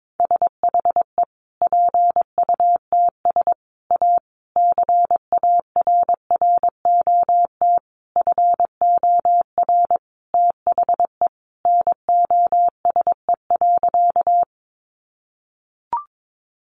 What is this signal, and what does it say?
Morse code, heard as a keyed tone, about 22 wpm, 700 Hz. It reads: SHE PUTS A CARROT FOR THE NOSE.
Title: Easy Phrases 22wpm